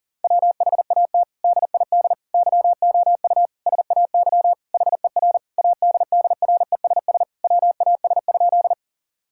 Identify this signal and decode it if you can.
Morse code, heard as a keyed tone, about 40 wpm, 700 Hz. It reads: WHAT DID YOU SAY HER ADDRESS WAS?
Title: Sentences from Top 1000 Words - Review Part 6 - 40wpm